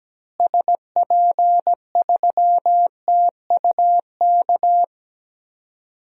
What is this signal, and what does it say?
Morse code, heard as a keyed tone, about 17 wpm, 700 Hz. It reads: SP3TUK
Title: DXCC-100 Call Signs - 6 Characters - Repeated 3x - 17wpm